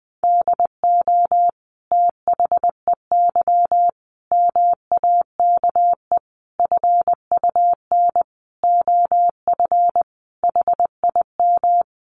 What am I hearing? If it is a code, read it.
Morse code, heard as a keyed tone, about 20 wpm, 700 Hz. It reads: DO THEY MAKE FUN OF HIM
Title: Sentences from Top 1300 Words 20wpm